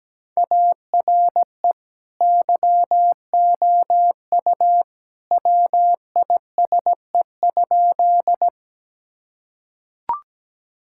Morse code, heard as a keyed tone, about 17 wpm, 700 Hz. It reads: ARE YOU WISE?
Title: Sentences from Top 1600 Words 17wpm